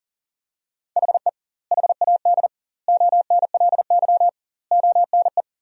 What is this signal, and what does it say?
Morse code, heard as a keyed tone, about 40 wpm, 700 Hz. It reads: HE HAD ONLY ONE
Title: 2x Word-Spacing - Sentences from Top 300 Words - 40wpm